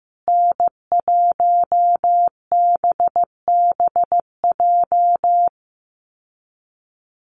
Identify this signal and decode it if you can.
Morse code, heard as a keyed tone, about 15 wpm, 700 Hz. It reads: N1BBJ
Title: USA Call Signs Repeated 3x 15wpm